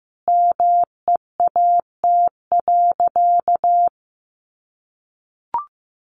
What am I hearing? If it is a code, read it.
Morse code, heard as a keyed tone, about 15 wpm, 700 Hz. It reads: MEAT.
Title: Sentences from Top 1500 Words 15wpm